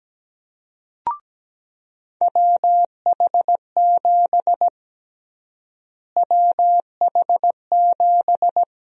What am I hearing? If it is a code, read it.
Morse code, heard as a keyed tone, about 17 wpm, 700 Hz. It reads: WH7 WH7
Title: USA Call Sign Prefixes - Repeated 2x - 17wpm